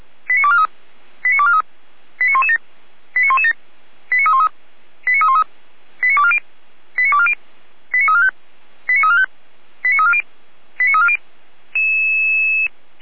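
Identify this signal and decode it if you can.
Five-tone selective call (ZVEI), heard as quick runs of five tones, tones of about 70 ms. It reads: 89243 89243 89117 89117 89313 89313 89240 89240 89345 89345 89340 89340
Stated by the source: ZVEI five-tone selective call 89514 received by SDR.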